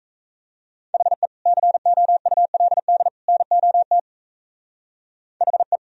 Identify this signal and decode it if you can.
Morse code, heard as a keyed tone, about 33 wpm, 700 Hz. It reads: HE COULD NOT HE
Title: Speed-Racing - Sets of 3 Words 28wpm